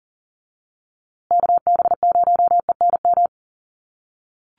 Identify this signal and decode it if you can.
Morse code, heard as a keyed tone, about 40 wpm, 700 Hz. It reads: KB0ENM
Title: USA Call Signs Repeated 2x 40wpm